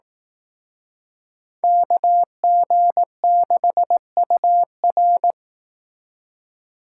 Morse code, heard as a keyed tone, about 18 wpm, 700 Hz. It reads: KG6UR
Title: Speed Racing - USA Call Signs 15wpm